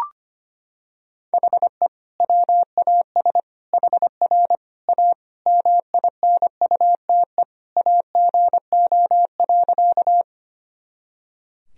Morse code, heard as a keyed tone, about 25 wpm, 700 Hz. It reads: HE WAS HR A MINUTE AGO.
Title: Sentences from Top 500 Words and Abbreviations 25wpm